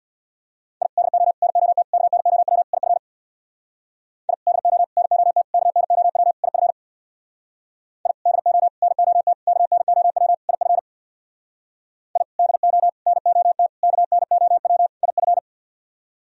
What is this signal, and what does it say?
Morse code, heard as a keyed tone, about 64 wpm, 700 Hz. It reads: I DO NOT KNOW IF I DO NOT KNOW IF I DO NOT KNOW IF I DO NOT KNOW IF